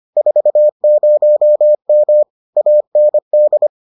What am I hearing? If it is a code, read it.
Morse code, heard as a keyed tone, about 25 wpm, 600 Hz. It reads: 40M AND